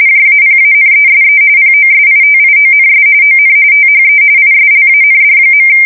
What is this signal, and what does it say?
RTTY, about 45.45 baud. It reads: JAPANESE POSITIONS ON THE ISLAND.
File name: fsk170 / USS-Missouri